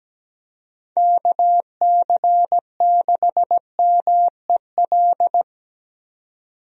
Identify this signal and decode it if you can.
Morse code, heard as a keyed tone, about 17 wpm, 700 Hz. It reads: KC6MEL
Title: USA Call Signs - 6 Characters - 17wpm